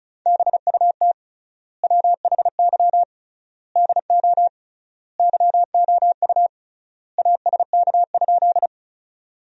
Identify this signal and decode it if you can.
Morse code, heard as a keyed tone, about 34 wpm, 700 Hz. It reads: BUT WHY DO YOU ASK?
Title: 3x Word-Spacing - Sentences from Top 200 Words - 35wpm